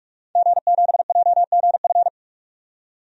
Morse code, heard as a keyed tone, about 41 wpm, 700 Hz. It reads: G7JGF
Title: DXCC-100 Call Signs - 5 Characters - 45wpm